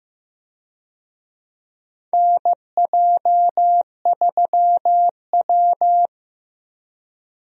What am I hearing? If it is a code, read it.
Morse code, heard as a keyed tone, about 15 wpm, 700 Hz. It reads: NJ3W
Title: USA Call Signs - 4 Characters - Repeated 2x - 15wpm